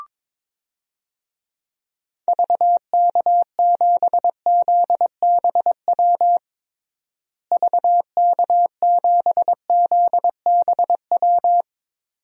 Morse code, heard as a keyed tone, about 22 wpm, 700 Hz. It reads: VK7ZBW VK7ZBW
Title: DXCC-100 Call Signs - 6 Characters - Repeated 2x - 22wpm